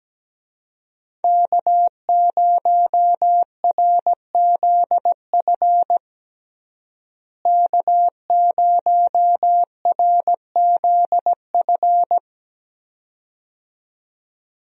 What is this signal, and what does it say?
Morse code, heard as a keyed tone, about 17 wpm, 700 Hz. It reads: K0RZF K0RZF
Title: USA Call Signs - 5 Characters - Repeated 2x - 17wpm